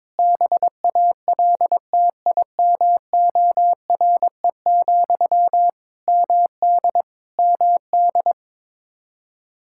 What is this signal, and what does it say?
Morse code, heard as a keyed tone, about 22 wpm, 700 Hz. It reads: BALTIMORE, MD MD
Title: Practice QSO - Exchange 1 - Beginning - 22wpm